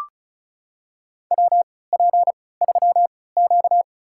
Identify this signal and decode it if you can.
Morse code, heard as a keyed tone, about 36 wpm, spaced as for 20 wpm, 700 Hz. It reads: WP3Q
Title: USA Call Signs - 4 Characters - 3x Character Spacing - 35wpm